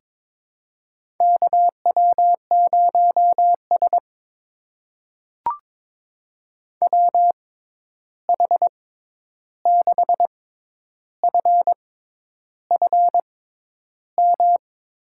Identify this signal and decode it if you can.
Morse code, heard as a keyed tone, about 22 wpm, spaced as for 10 wpm, 700 Hz. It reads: KW0S WH6FFM
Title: USA Call Signs - 6x Character Spacing - 22wpm